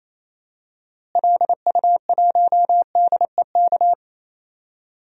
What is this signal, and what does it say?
Morse code, heard as a keyed tone, about 28 wpm, 700 Hz. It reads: LU1DEK
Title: DXCC-100 Call Signs - 6 Characters - Repeated 3x - 28wpm